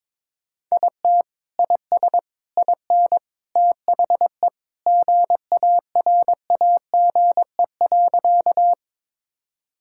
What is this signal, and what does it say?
Morse code, heard as a keyed tone, about 22 wpm, 700 Hz. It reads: IT IS IN THE GARAGE.
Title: Sentences from Top 1600 Words 22wpm